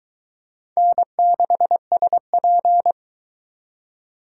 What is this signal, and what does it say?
Morse code, heard as a keyed tone, about 22 wpm, 700 Hz. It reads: N6SP